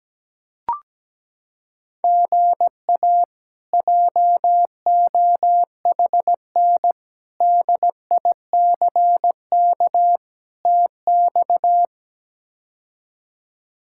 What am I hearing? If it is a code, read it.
Morse code, heard as a keyed tone, about 17 wpm, 700 Hz. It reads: GA JOHN DICK TX
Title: K1USN SST QSO Practice - 17wpm